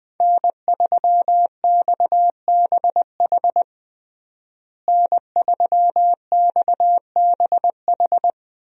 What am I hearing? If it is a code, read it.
Morse code, heard as a keyed tone, about 21 wpm, 700 Hz. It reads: N3XBH N3XBH
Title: USA Call Signs - 5 Characters - Repeated 2x - 20wpm